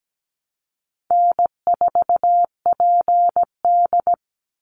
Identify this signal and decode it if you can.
Morse code, heard as a keyed tone, about 17 wpm, 700 Hz. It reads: N4PD
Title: Call Signs Repeated 2x - 17wpm